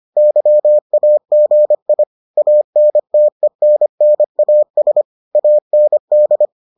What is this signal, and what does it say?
Morse code, heard as a keyed tone, about 25 wpm, 600 Hz. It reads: YAGI ANTENNAS AND